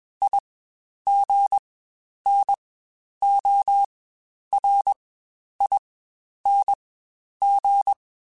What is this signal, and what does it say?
Morse code, heard as a keyed tone, about 21 wpm, spaced as for 10 wpm, 800 Hz. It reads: IGNORING
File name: Quotes Spaced / Quotes Spaced-mp3s / IgnoringFacts sp @20